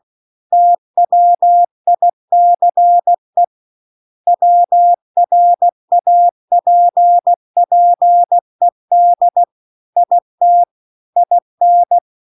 Morse code, heard as a keyed tone, about 16 wpm, 700 Hz. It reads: TWICE WRAPPED IT IN